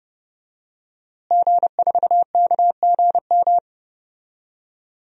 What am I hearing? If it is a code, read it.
Morse code, heard as a keyed tone, about 30 wpm, 700 Hz. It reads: G4KGM
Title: DXCC-100 Call Signs - 5 Characters - Repeated 2x - 30wpm